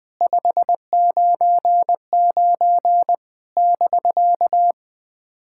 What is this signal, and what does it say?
Morse code, heard as a keyed tone, about 20 wpm, 700 Hz. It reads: 599 <BK>
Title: POTA QSO Practice - 20wpm